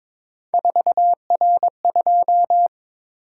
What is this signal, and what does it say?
Morse code, heard as a keyed tone, about 22 wpm, 700 Hz. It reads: 4R2
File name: practice / 22wpm / Three Letter-Number Combinations - 22wpm